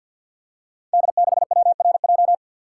Morse code, heard as a keyed tone, about 52 wpm, 700 Hz. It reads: D6WRJ
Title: Code Groups - 5 Letter:Numbers - 50wpm